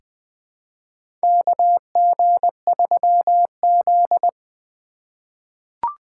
Morse code, heard as a keyed tone, about 20 wpm, 700 Hz. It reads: KG3Z
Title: USA Call Signs - 4 Characters - 4x Character Spacing - 20wpm